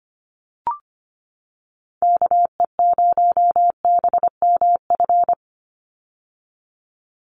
Morse code, heard as a keyed tone, about 25 wpm, 690 Hz. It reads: KE0BMF
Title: USA Call Signs 25wpm